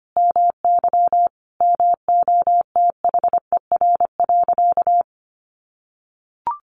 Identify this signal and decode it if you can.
Morse code, heard as a keyed tone, about 25 wpm, 700 Hz. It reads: MY MOTHER.
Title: Sentences from Top 600 Words 25wpm